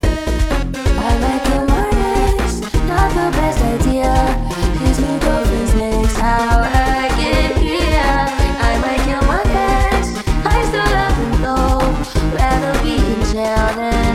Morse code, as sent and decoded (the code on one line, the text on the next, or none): none